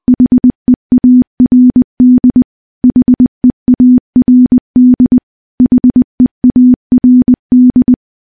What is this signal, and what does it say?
Morse code, heard as a keyed tone, about 20 wpm, 250 Hz. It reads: HEARD HEARD HEARD
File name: morse / morse-training-20WPM